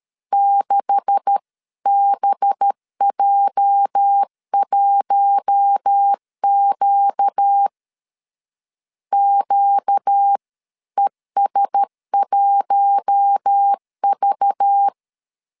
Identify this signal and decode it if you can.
Morse code, heard as a keyed tone, about 13 wpm, 800 Hz. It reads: 6BJ1Q QES1V
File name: CW / DET / 1 9 1500C S A 12wpm 3800